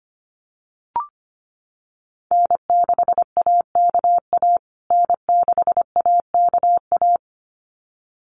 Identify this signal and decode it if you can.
Morse code, heard as a keyed tone, about 25 wpm, 700 Hz. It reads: N6AKA N6AKA
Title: USA Call Signs Repeated 2x 25wpm